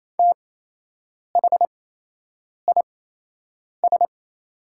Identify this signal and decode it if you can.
Morse code, heard as a keyed tone, about 28 wpm, spaced as for 8 wpm, 700 Hz. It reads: THIS
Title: 8x Character Spacing - Top 100 Words - 28wpm